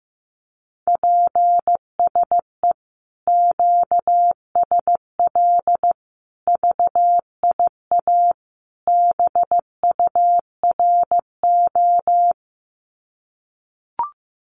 Morse code, heard as a keyed tone, about 15 wpm, 700 Hz. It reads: PSE QSL VIA BURO
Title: Common QSO Phrases 15wpm